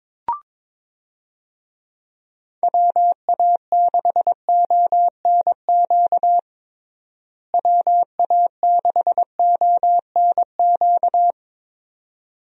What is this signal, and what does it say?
Morse code, heard as a keyed tone, about 22 wpm, 700 Hz. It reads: WA6ONQ WA6ONQ
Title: USA Call Signs - 6 Characters - Repeated 2x - 22wpm